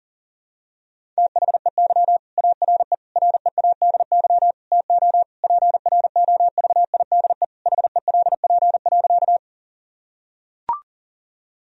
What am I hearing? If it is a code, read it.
Morse code, heard as a keyed tone, about 40 wpm, 700 Hz. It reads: THEY ARE READY TO PROVIDE HELP.
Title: Sentences from Top 1000 Words - Review Part 3 - 40wpm